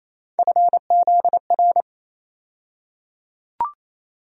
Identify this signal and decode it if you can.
Morse code, heard as a keyed tone, about 28 wpm, 700 Hz. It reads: FZR
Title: USA Call Sign Postfixes - 3 Letters 28wpm